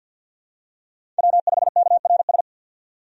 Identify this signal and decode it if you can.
Morse code, heard as a keyed tone, about 48 wpm, 700 Hz. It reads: W5KRS
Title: USA Call Signs - 5 Characters - 50wpm